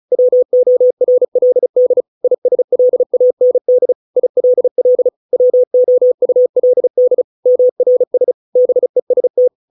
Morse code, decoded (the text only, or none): WORLD ISLAND ILL WOULD MRS BEST